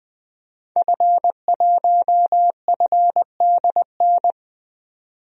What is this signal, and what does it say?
Morse code, heard as a keyed tone, about 20 wpm, 700 Hz. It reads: F1FDN
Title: DXCC-100 Call Signs - 5 Characters - 20wpm